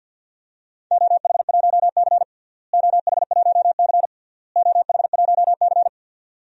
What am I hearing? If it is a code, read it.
Morse code, heard as a keyed tone, about 52 wpm, 700 Hz. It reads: OH1C OH1C OH1C
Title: DXCC-100 Call Signs - 4 Characters - Repeated 3x - 50wpm